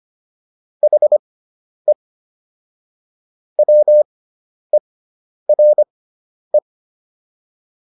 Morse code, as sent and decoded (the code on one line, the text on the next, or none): .... / .-- . .-. .
HE WERE